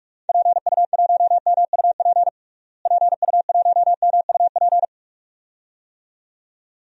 Morse code, heard as a keyed tone, about 42 wpm, 700 Hz. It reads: PU1MUP PU1MUP
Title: DXCC-100 Call Signs - 6 Characters - Repeated 2x - 45wpm